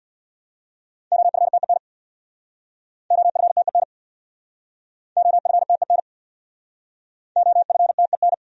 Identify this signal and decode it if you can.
Morse code, heard as a keyed tone, about 64 wpm, 700 Hz. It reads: OFTEN OFTEN OFTEN OFTEN